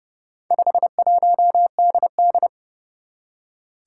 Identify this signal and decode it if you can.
Morse code, heard as a keyed tone, about 30 wpm, 700 Hz. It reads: 51DD
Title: Four Challenging Letter-Number Combinations 30wpm